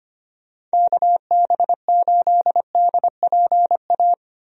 Morse code, heard as a keyed tone, about 25 wpm, 700 Hz. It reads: KB8DPA
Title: USA Call Signs - 6 Characters - Repeated 2x - 25wpm